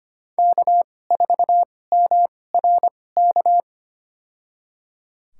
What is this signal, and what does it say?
Morse code, heard as a keyed tone, about 25 wpm, spaced as for 18 wpm, 700 Hz. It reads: K4MRK
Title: USA Call Signs - 5 Characters - 2x Character Spacing - 25wpm